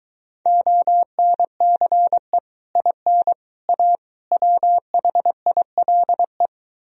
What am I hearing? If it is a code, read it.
Morse code, heard as a keyed tone, about 23 wpm, 700 Hz. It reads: ONCE IN A WHILE